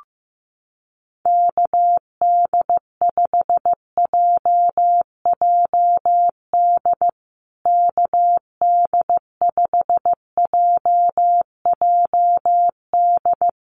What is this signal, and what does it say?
Morse code, heard as a keyed tone, about 15 wpm, 700 Hz. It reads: KD5JJD KD5JJD
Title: USA Call Signs Repeated 3x 15wpm